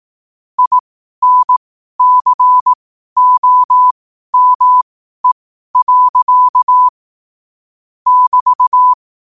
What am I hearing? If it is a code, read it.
Morse code, heard as a keyed tone, about 18 wpm, spaced as for 12 wpm, 1000 Hz. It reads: INCOME. <BT>
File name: 12wpm